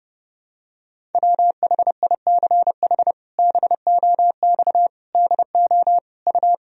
Morse code, heard as a keyed tone, about 30 wpm, 700 Hz. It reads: WHICH BOX DO U
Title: Sentences from Top 500 Words and Abbreviations 30wpm